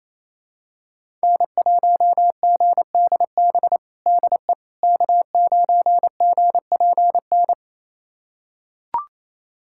Morse code, heard as a keyed tone, about 28 wpm, 700 Hz. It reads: N1GDB DE K9GPN
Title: USA Callsign DE Callsign - 5 Characters - 28wpm